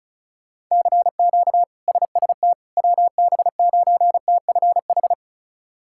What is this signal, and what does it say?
Morse code, heard as a keyed tone, about 36 wpm, 700 Hz. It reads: CQ SST WB9TFH
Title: K1USN SST QSO Practice - 35wpm